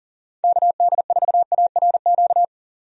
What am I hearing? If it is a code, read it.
Morse code, heard as a keyed tone, about 40 wpm, 700 Hz. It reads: KD4ARQ
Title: USA Call Signs - 6 Characters - 40wpm